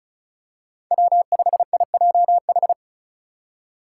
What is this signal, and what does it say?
Morse code, heard as a keyed tone, about 34 wpm, 700 Hz. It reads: W5IJH
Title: K1USN SST QSO Practice - 35wpm